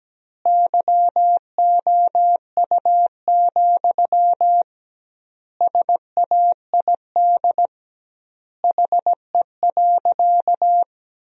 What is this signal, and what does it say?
Morse code, heard as a keyed tone, about 17 wpm, 700 Hz. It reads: YOU, SAID HE.
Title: 2x Word-Spacing - Sentences from Top 400 Words - 17wpm